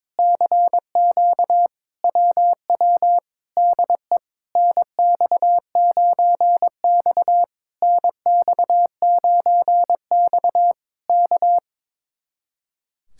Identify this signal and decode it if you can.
Morse code, heard as a keyed tone, about 22 wpm, 700 Hz. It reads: CQ WW DE NX9X NX9X K
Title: Speed-Racing Common QSO Phrases - 22wpm